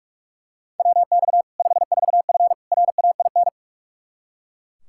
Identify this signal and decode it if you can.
Morse code, heard as a keyed tone, about 41 wpm, 700 Hz. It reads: WX 54F RAIN
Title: Practice QSO - Exchange 2 - Info - 45wpm